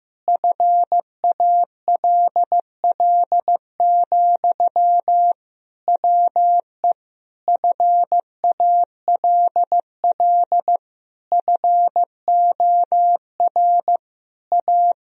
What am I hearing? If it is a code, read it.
Morse code, heard as a keyed tone, about 15 wpm, 700 Hz. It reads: FALL, WE FALL FOR A
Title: Sentences from Top 1000 Words - Review Part 2 - 15wpm